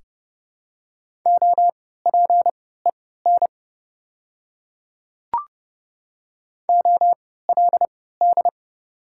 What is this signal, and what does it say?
Morse code, heard as a keyed tone, about 30 wpm, spaced as for 17 wpm, 700 Hz. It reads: OPEN OLD
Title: 3x Character Spacing - Top 500 Words - 30wpm